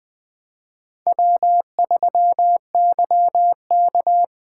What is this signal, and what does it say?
Morse code, heard as a keyed tone, about 20 wpm, 700 Hz. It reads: W3YK
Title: Speed Racing - USA Call Signs 17wpm